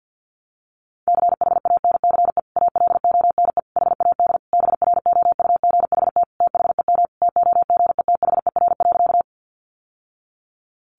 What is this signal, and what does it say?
Morse code, heard as a keyed tone, about 50 wpm, 700 Hz. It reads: CHANCE ALONE HAD BROUGHT THEM TOGETHER.
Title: Sentences from Top 1000 Words 50wpm